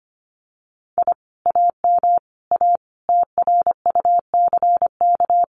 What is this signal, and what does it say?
Morse code, heard as a keyed tone, about 25 wpm, 700 Hz. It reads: I AM A TRUCK
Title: Common QSO Phrases Encore 25wpm